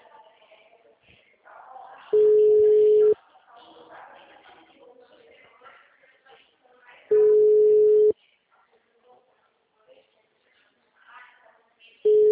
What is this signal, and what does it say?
Ringback tone (Europe, 425 Hz), with speech.
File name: out- -2018- - - .3405